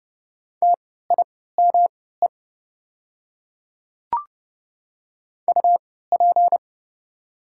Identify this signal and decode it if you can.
Morse code, heard as a keyed tone, about 30 wpm, spaced as for 17 wpm, 700 Hz. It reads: TIME UP
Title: 3x Character Spacing - Top 100 Words - 30wpm